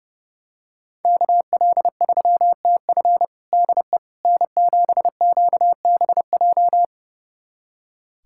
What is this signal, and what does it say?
Morse code, heard as a keyed tone, about 30 wpm, 700 Hz. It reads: KL3TF DE N7QBJ
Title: USA Callsign DE Callsign - 5 Characters - 30wpm